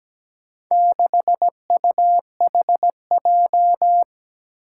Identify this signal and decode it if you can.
Morse code, heard as a keyed tone, about 17 wpm, 700 Hz. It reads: 6UHJ
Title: Four Challenging Letter-Number Combinations 17wpm